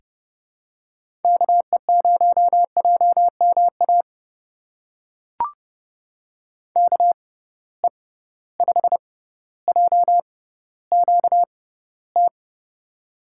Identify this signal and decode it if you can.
Morse code, heard as a keyed tone, about 30 wpm, spaced as for 15 wpm, 700 Hz. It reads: KE0JMA KE5JQT
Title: USA Call Signs - 6 Characters - 6x Character Spacing - 30wpm